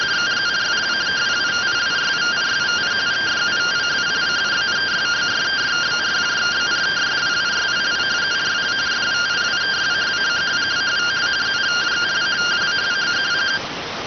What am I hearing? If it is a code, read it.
Two-tone teleprinter signal, about 170 Hz shift.